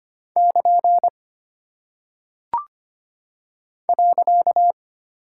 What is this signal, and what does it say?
Morse code, heard as a keyed tone, about 25 wpm, 700 Hz. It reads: <KN> .
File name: Single Punctuation-Prosign - 25wpm